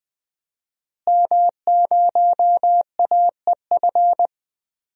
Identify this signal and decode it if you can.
Morse code, heard as a keyed tone, about 20 wpm, 700 Hz. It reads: M0AEF
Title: DXCC-100 Call Signs - 5 Characters - Repeated 2x - 20wpm